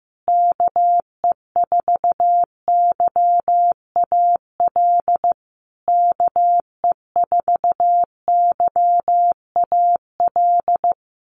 Morse code, heard as a keyed tone, about 15 wpm, 700 Hz. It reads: KE4YAL KE4YAL
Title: USA Call Signs Repeated 3x 15wpm